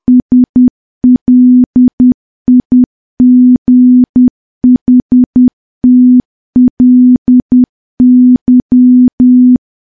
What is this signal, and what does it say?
Morse code, heard as a keyed tone, about 10 wpm, 250 Hz. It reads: SLIGHTLY
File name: morse / morse-training-10WPM